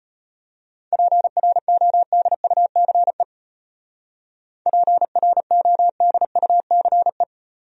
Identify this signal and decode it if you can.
Morse code, heard as a keyed tone, about 32 wpm, 700 Hz. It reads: PRODUCE PRODUCE